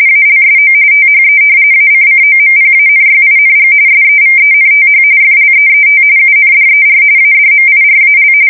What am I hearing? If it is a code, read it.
RTTY, about 45.45 baud. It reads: ALSO THE ACCOMPANYING SEVENTH FLEET UNDER ADMIRAL
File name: fsk170 / battleshipnj